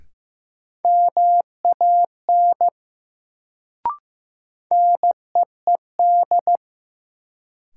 Morse code, heard as a keyed tone, about 15 wpm, 700 Hz. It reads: MAN NEED
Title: Sentences from Top 200 Words - New Word Drill - 15wpm